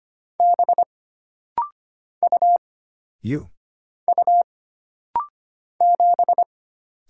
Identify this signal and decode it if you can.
Morse code, heard as a keyed tone, about 25 wpm, 700 Hz. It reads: B U U 7
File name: Single Character (Letter:Number:Punctuation:Prosign) - Rapid-Fire - 25wpm